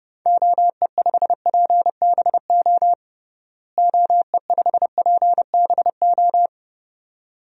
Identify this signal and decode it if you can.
Morse code, heard as a keyed tone, about 30 wpm, 700 Hz. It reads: OE5PBO OE5PBO
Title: DXCC-100 Call Signs - 6 Characters - Repeated 2x - 30wpm